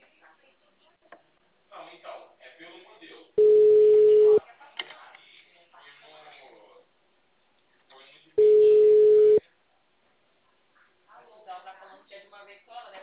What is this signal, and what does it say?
Ringback tone (Europe, 425 Hz), with speech.